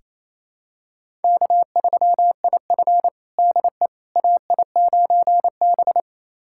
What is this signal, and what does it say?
Morse code, heard as a keyed tone, about 27 wpm, 700 Hz. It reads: K3IF DE AI9B
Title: USA Callsign DE Callsign - 4 Characters - 28wpm